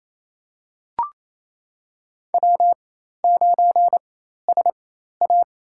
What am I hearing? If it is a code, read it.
Morse code, heard as a keyed tone, about 28 wpm, spaced as for 13 wpm, 700 Hz. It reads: W9SA
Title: USA Call Signs - 4x Character Spacing - 28wpm